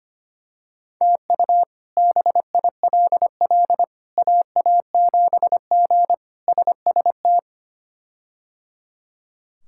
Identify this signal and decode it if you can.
Morse code, heard as a keyed tone, about 25 wpm, 700 Hz. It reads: TU BILL AA7G SST
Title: K1USN SST QSO Practice - 25wpm